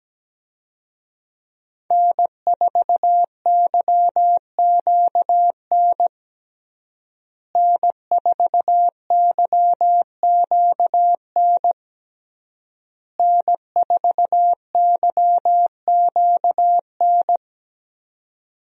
Morse code, heard as a keyed tone, about 17 wpm, 700 Hz. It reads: N4YQN N4YQN N4YQN
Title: USA Call Signs - 5 Characters - Repeated 3x - 17wpm